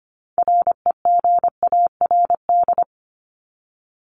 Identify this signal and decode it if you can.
Morse code, heard as a keyed tone, about 25 wpm, 700 Hz. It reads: REGARD
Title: ING Suffix - Top 100 Words with Base Words 25wpm